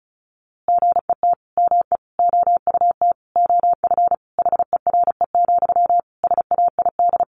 Morse code, heard as a keyed tone, about 34 wpm, 700 Hz. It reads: GET ME OUT OF HERE, SAID
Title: A Sound Of Thunder 35wpm